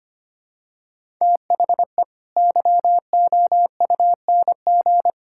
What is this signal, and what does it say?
Morse code, heard as a keyed tone, about 25 wpm, 700 Hz. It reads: THE YOUNG
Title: Sentences from Top 1000 Words - Review Part 6 - 25wpm